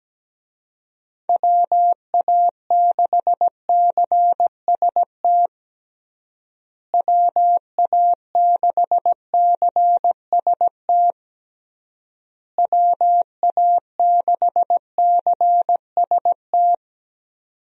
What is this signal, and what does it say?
Morse code, heard as a keyed tone, about 17 wpm, 700 Hz. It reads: WA6CST WA6CST WA6CST
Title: USA Call Signs - 6 Characters - Repeated 3x - 17wpm